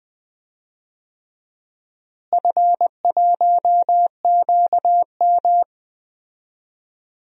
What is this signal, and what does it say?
Morse code, heard as a keyed tone, about 20 wpm, 700 Hz. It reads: F1QM
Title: DXCC-100 Call Signs - 4 Characters - Repeated 3x - 20wpm